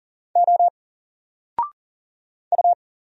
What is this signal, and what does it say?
Morse code, heard as a keyed tone, about 40 wpm, 700 Hz. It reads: O U
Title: Single Letter-Number Alt 40wpm